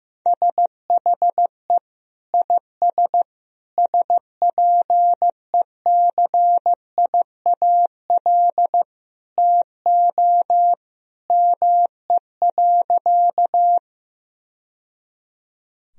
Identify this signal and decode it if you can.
Morse code, heard as a keyed tone, about 16 wpm, 700 Hz. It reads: SHE IS SPECIAL TO ME.
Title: Sentences from Top 1000 Words - Review Part 3 - 15wpm